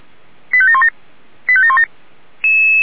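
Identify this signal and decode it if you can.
Five-tone selective call (ZVEI), heard as quick runs of five tones, tones of about 70 ms. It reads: 85717 85717
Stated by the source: ZVEI five-tone selective call 85730 received by SDR.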